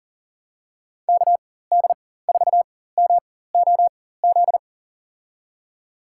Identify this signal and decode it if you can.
Morse code, heard as a keyed tone, about 40 wpm, spaced as for 19 wpm, 700 Hz. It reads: KD4MOZ
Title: USA Call Signs - 6 Characters - 4x Character Spacing - 40wpm